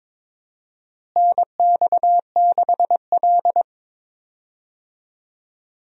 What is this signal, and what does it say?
Morse code, heard as a keyed tone, about 22 wpm, 700 Hz. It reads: NX6L
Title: USA Call Signs - 4 Characters - Repeated 2x - 22wpm